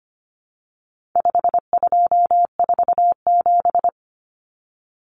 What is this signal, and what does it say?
Morse code, heard as a keyed tone, about 25 wpm, 700 Hz. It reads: 5247
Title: Common Cut Numbers - 4-Digit 25wpm